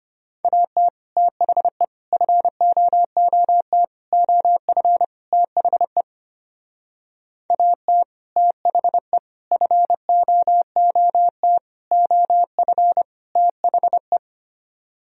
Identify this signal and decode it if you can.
Morse code, heard as a keyed tone, about 32 wpm, 700 Hz. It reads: AT THE FOOT OF THE AT THE FOOT OF THE